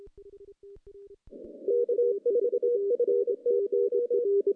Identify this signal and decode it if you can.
Morse code, heard as a keyed tone, about 27 wpm, 480 Hz. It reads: K4FTO
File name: contests / CW-SS-k5tr- -0000